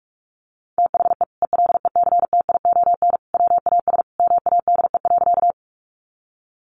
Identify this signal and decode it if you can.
Morse code, heard as a keyed tone, about 44 wpm, 700 Hz. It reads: THE ELECTION WAS MADE.
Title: Sentences from Top 1500 Words 45wpm